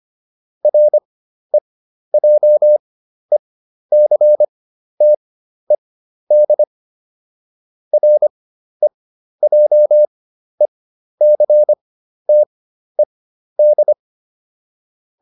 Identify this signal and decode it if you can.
Morse code, heard as a keyed tone, about 25 wpm, spaced as for 12 wpm, 600 Hz. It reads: REJECTED REJECTED
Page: fundamental-practice-files